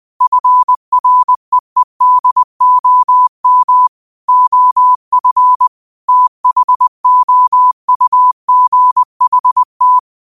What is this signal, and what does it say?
Morse code, heard as a keyed tone, about 20 wpm, 1000 Hz. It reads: FREEDOM OF THOUGHT